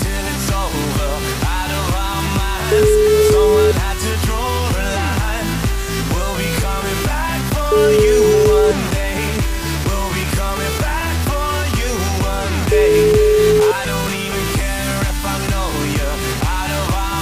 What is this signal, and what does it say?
Ringback tone (Europe, 425 Hz), with noise.